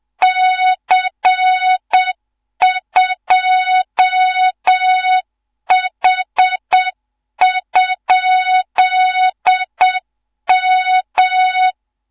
Morse code, heard as a keyed tone, about 7 wpm, 3700 Hz, with noise. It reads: C2H?M
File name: Morze-HA2MN / hang / 15 GYAKORLAT